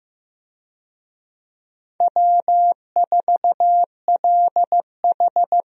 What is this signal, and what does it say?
Morse code, heard as a keyed tone, about 15 wpm, 700 Hz. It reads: W4LH
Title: USA Call Signs - 4 Characters - Repeated 2x - 15wpm